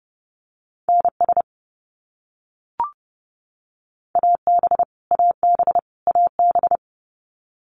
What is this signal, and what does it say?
Morse code, heard as a keyed tone, about 30 wpm, 700 Hz. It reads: NS AB AB AB
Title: Canadian Provinces and Territories - Repeated 3x - 30wpm